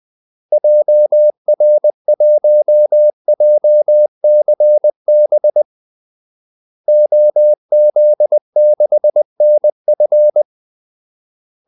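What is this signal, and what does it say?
Morse code, heard as a keyed tone, about 20 wpm, 600 Hz. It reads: JR1JCB OZ6NF
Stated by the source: Morse call-sign practice at 20 wpm.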